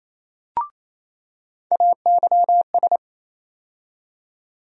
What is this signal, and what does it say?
Morse code, heard as a keyed tone, about 28 wpm, 700 Hz. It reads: AYS
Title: USA Call Sign Postfixes - 3 Letters 28wpm